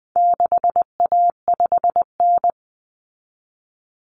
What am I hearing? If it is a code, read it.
Morse code, heard as a keyed tone, about 20 wpm, 700 Hz. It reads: 6A5N
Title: Common Cut Numbers - 4-Digit 20wpm